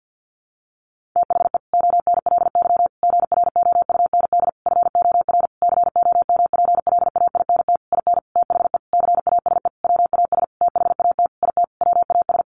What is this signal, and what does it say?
Morse code, heard as a keyed tone, about 50 wpm, 700 Hz. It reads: THE ONLY GROUND FOR COMPLAINT IN THE CASE WAS THAT IT WAS
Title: Sentences from Top 1400 Words 50wpm